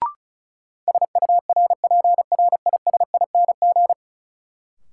Morse code, heard as a keyed tone, about 36 wpm, 700 Hz. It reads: SURPRISING
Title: Sets of 5 Words - New Word Drill - 35wpm